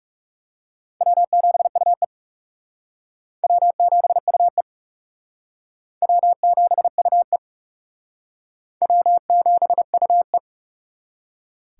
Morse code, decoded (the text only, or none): W7UE W7UE W7UE W7UE